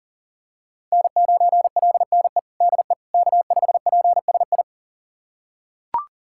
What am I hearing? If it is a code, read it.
Morse code, heard as a keyed tone, about 37 wpm, 700 Hz. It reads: N9LNE DE K5PSI
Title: USA Callsign DE Callsign - 5 Characters - 40wpm